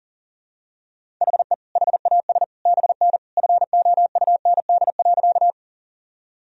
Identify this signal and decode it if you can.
Morse code, heard as a keyed tone, about 40 wpm, 700 Hz. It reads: HE HAS BN FOUND.
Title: Sentences from Top 300 Words and Abbreviations 40wpm